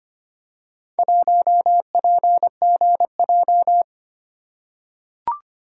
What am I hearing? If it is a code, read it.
Morse code, heard as a keyed tone, about 25 wpm, 700 Hz. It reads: 1PGJ
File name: practice / 25wpm / Four Letter-Number Combinations - 25wpm